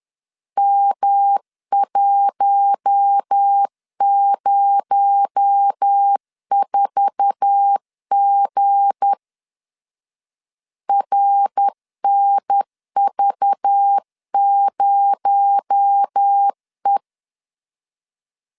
Morse code, heard as a keyed tone, about 10 wpm, 800 Hz. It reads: M104G RNV0E
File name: LEZIONI / 9 2867 10wpm 2733